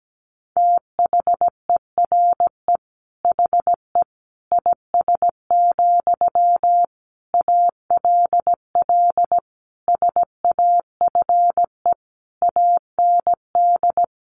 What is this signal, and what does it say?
Morse code, heard as a keyed tone, about 17 wpm, 700 Hz. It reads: THERE HE IS, ALL SAFE AND
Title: Sentences from Top 900 Words 17wpm